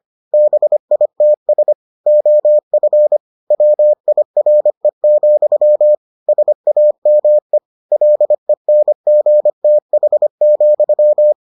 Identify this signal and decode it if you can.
Morse code, heard as a keyed tone, about 25 wpm, 600 Hz. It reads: BITS OF WIRE, SAME LENGTH,